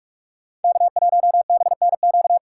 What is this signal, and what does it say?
Morse code, heard as a keyed tone, about 46 wpm, 700 Hz. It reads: K1BNQ
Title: USA Call Signs - 5 Characters - 2x Character Spacing - 45wpm